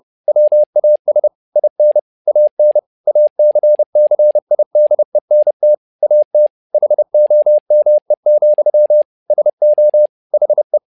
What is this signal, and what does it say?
Morse code, heard as a keyed tone, about 30 wpm, 600 Hz. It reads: WAS IN AN ACCIDENT AT HOME, SO HE
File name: learning center / tower trouble / tower trouble 30wpm